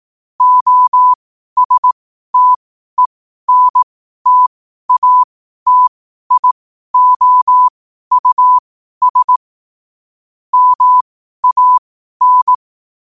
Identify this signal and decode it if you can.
Morse code, heard as a keyed tone, about 18 wpm, spaced as for 12 wpm, 1000 Hz. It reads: OSTENTATIOUS MAN
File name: 12wpm